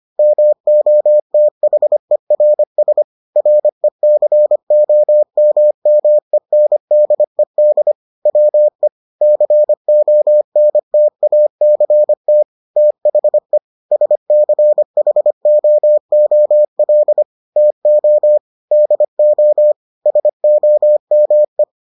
Morse code, heard as a keyed tone, about 25 wpm, 600 Hz. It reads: MOTHERS RECOMMENDED WE CONTACT THE SCHOOL TO DO SOME